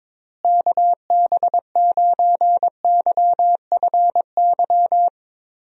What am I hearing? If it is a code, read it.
Morse code, heard as a keyed tone, about 22 wpm, 700 Hz. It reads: KB9YFY
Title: USA Call Signs - 6 Characters - Repeated 3x - 22wpm